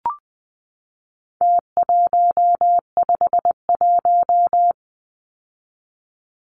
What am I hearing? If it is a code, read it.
Morse code, heard as a keyed tone, about 20 wpm, 700 Hz. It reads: T151
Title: Numbers - Contest Style - 4-Digit 20wpm